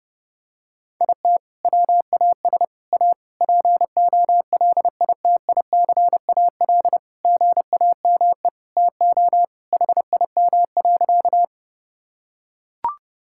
Morse code, heard as a keyed tone, about 32 wpm, 700 Hz. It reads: IT WAS A POLITICAL GAME TO HIM.
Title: Sentences from Top 1000 Words - Review Part 5 - 30wpm